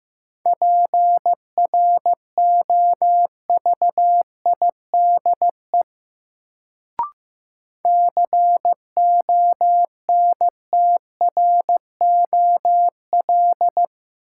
Morse code, heard as a keyed tone, about 15 wpm, 700 Hz. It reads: PROVIDE CONTROL
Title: Sentences from Top 1000 Words - Review Part 2 - New Word Drill - 15wpm